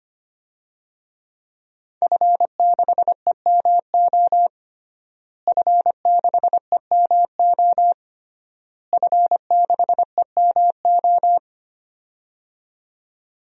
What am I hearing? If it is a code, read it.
Morse code, heard as a keyed tone, about 25 wpm, 700 Hz. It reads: F6EMO F6EMO F6EMO
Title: DXCC-100 Call Signs - 5 Characters - Repeated 3x - 25wpm